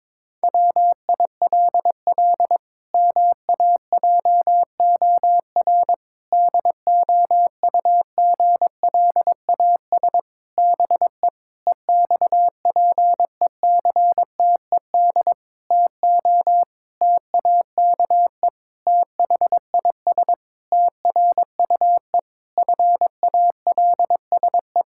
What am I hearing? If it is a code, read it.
Morse code, heard as a keyed tone, about 22 wpm, 700 Hz. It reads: WILL MAJOR DOUGLAS BE EXPECTED TO TAKE THIS TRUE FALSE
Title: Pangram Challenge v2 - 22wpm